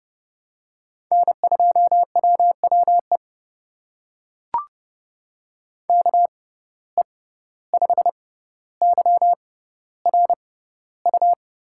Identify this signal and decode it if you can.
Morse code, heard as a keyed tone, about 30 wpm, spaced as for 15 wpm, 700 Hz. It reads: N2WWE KE5YRU
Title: USA Call Signs - 6x Character Spacing - 30wpm